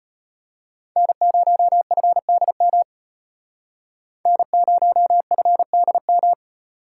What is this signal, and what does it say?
Morse code, heard as a keyed tone, about 36 wpm, 700 Hz. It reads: N0FDM N0FDM